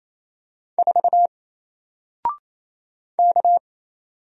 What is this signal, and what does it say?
Morse code, heard as a keyed tone, about 27 wpm, 700 Hz. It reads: 4 K
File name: Single-Letter-Number-28wpm